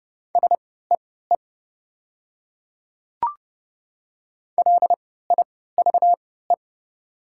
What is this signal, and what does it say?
Morse code, heard as a keyed tone, about 31 wpm, spaced as for 17 wpm, 700 Hz. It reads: SEE LIVE
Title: 3x Character Spacing - Top 500 Words - 30wpm